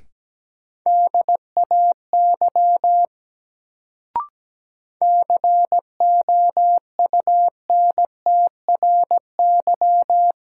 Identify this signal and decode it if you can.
Morse code, heard as a keyed tone, about 17 wpm, 700 Hz. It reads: DAY COUNTRY
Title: Sets of 2 Words - New Word Drill - 17wpm